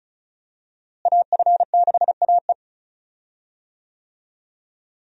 Morse code, heard as a keyed tone, about 36 wpm, 700 Hz. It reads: AF6AE
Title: USA Call Signs - 5 Characters - Repeated 3x - 35wpm